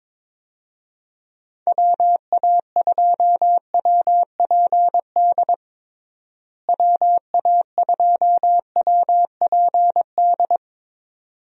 Morse code, heard as a keyed tone, about 22 wpm, 700 Hz. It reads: WA2WPD WA2WPD
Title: USA Call Signs - 6 Characters - Repeated 2x - 22wpm